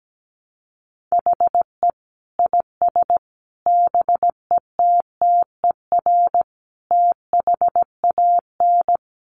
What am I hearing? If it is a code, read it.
Morse code, heard as a keyed tone, about 17 wpm, 700 Hz. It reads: HE IS BETTER THAN
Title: Sentences from Top 400 Words 17wpm